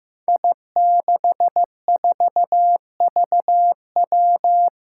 Morse code, heard as a keyed tone, about 15 wpm, 700 Hz. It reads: I64VW
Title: Five Challenging Letter-Number Combinations 15wpm